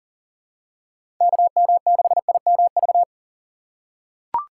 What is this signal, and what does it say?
Morse code, heard as a keyed tone, about 40 wpm, 700 Hz. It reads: KM6IMV
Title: USA Call Signs - 6 Characters - 6x Character Spacing - 40wpm